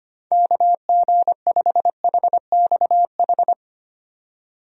Morse code, heard as a keyed tone, about 25 wpm, 700 Hz. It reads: KG5HXH
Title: USA Call Signs - 6 Characters - 25wpm